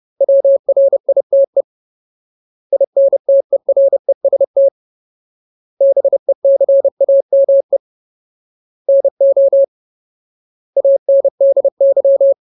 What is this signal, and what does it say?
Morse code, heard as a keyed tone, about 30 wpm, 550 Hz. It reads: WRITE INTEREST BECAME NO ANDY
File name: Ultimate Word List / 30WPM Ultimate